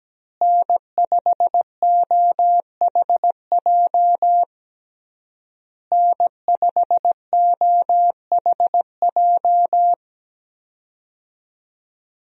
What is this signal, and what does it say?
Morse code, heard as a keyed tone, about 17 wpm, 700 Hz. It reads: N5OHJ N5OHJ
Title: USA Call Signs - 5 Characters - Repeated 3x - 17wpm